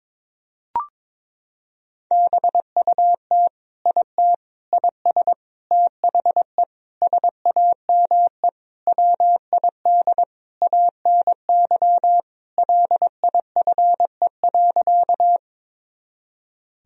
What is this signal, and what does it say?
Morse code, heard as a keyed tone, about 22 wpm, 700 Hz. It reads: BUT IT IS THE SAME WID ANY LIFE.
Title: Sentences from Top 300 Words and Abbreviations 22wpm